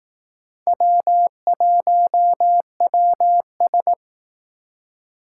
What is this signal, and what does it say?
Morse code, heard as a keyed tone, about 18 wpm, 700 Hz. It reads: W1WS
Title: Speed Racing - USA 1x2 Call Signs 15wpm